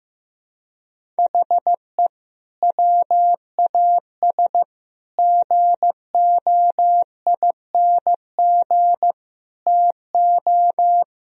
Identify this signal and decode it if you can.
Morse code, heard as a keyed tone, about 15 wpm, 700 Hz. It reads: HE WAS GOING TO
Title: Speed-Racing - Sets of 5 Words 15wpm